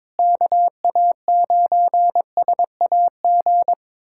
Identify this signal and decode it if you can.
Morse code, heard as a keyed tone, about 22 wpm, 700 Hz. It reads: KA9SAG
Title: USA Callsign DE Callsign - 6 Characters - 22wpm